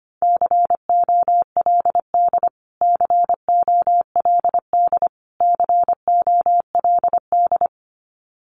Morse code, heard as a keyed tone, about 25 wpm, 700 Hz. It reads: COLD COLD COLD
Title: Top 500 Words Repeated 25wpm